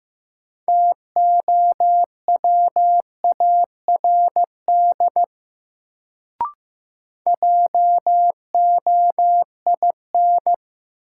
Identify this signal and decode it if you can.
Morse code, heard as a keyed tone, about 15 wpm, 700 Hz. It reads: TOWARD JOIN
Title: Sentences from Top 1000 Words - Review Part 6 - New Word Drill - 15wpm